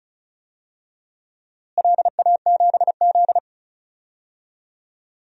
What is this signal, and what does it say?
Morse code, heard as a keyed tone, about 36 wpm, 700 Hz. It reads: LA7Z
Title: DXCC-100 Call Signs - 4 Characters - Repeated 3x - 35wpm